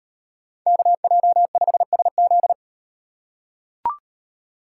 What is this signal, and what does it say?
Morse code, heard as a keyed tone, about 38 wpm, 700 Hz. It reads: KJ5SZ